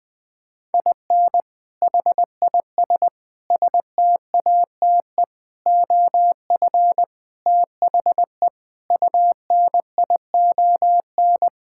Morse code, heard as a keyed tone, about 21 wpm, 700 Hz. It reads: IN HIS STATE OF THE UNION
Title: Sentences with 7-Word Phrases - Spoken Beginning - 20wpm